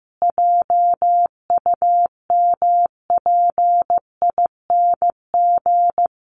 Morse code, heard as a keyed tone, about 15 wpm, 700 Hz. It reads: JUMPING
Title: Easy Phrases 15wpm